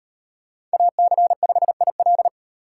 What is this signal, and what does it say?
Morse code, heard as a keyed tone, about 37 wpm, 700 Hz. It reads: AC5IL